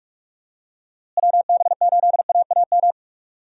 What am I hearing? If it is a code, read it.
Morse code, heard as a keyed tone, about 48 wpm, 700 Hz. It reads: WB8AAM
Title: USA Call Signs - 6 Characters - Repeated 2x - 45wpm